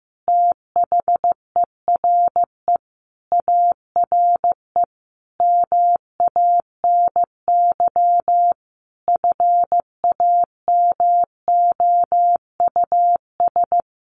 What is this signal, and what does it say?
Morse code, heard as a keyed tone, about 15 wpm, 700 Hz. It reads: THERE ARE MANY FAMOUS
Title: Sentences from Top 1500 Words 15wpm